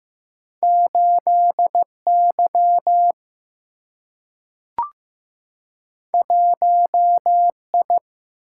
Morse code, heard as a keyed tone, about 15 wpm, 700 Hz. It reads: 8Y 1I
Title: Single-Letter Single-Number 15wpm